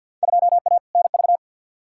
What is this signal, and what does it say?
Morse code, heard as a keyed tone, about 53 wpm, 700 Hz. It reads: 2A NV
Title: Field Day Exchanges - 50wpm